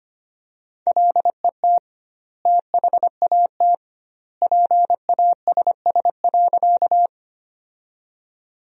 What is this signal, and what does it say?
Morse code, heard as a keyed tone, about 25 wpm, 700 Hz. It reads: LET THAT PASS.
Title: 2x Word-Spacing - Sentences from Top 400 Words - 25wpm